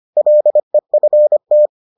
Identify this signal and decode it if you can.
Morse code, heard as a keyed tone, about 25 wpm, 600 Hz. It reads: LEFT